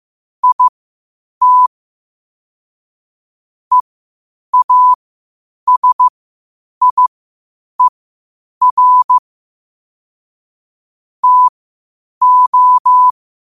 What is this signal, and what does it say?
Morse code, heard as a keyed tone, about 15 wpm, spaced as for 8 wpm, 1000 Hz. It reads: IT EASIER TO
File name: qotdfiles / 8wpm